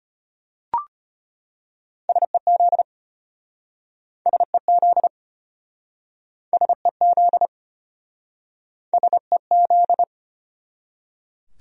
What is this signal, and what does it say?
Morse code, heard as a keyed tone, about 34 wpm, 700 Hz. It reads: SEZ SEZ SEZ SEZ